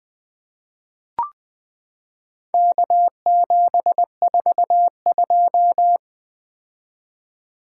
Morse code, heard as a keyed tone, about 20 wpm, 700 Hz. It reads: K742
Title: Four Letter-Number Combinations - 20wpm